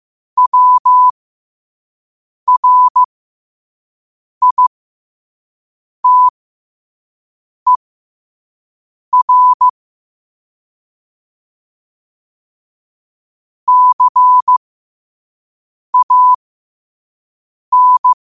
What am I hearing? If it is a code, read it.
Morse code, heard as a keyed tone, about 15 wpm, spaced as for 5 wpm, 1000 Hz. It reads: WRITER CAN
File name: qotdfiles / 5wpm